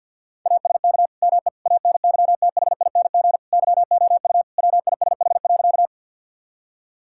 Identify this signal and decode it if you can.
Morse code, heard as a keyed tone, about 50 wpm, 700 Hz. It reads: ASK ME ANYTHING YOU WISH.
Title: Sentences from Top 1600 Words 50wpm